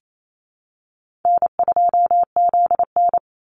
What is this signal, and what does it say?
Morse code, heard as a keyed tone, about 28 wpm, 700 Hz. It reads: N2ZN
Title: Call Signs Repeated 2x - 28wpm